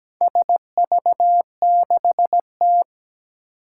Morse code, heard as a keyed tone, about 17 wpm, 700 Hz. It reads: SV6T
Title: Four Challenging Letter-Number Combinations 17wpm